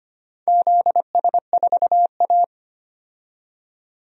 Morse code, heard as a keyed tone, about 25 wpm, 700 Hz. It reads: ZS4A
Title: DXCC-100 Call Signs - 4 Characters - 25wpm